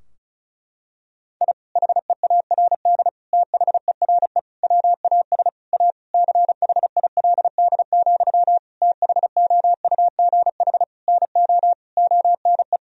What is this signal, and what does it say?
Morse code, heard as a keyed tone, about 36 wpm, 700 Hz. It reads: I HEARD THERE WAS A CHILD, THOUGH NO ONE
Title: Sentences from Top 1000 Words - Review Part 2 - 35wpm